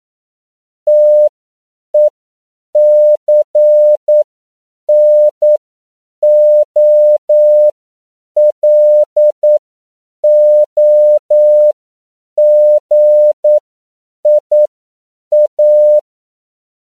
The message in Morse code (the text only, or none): TECNOLOGIA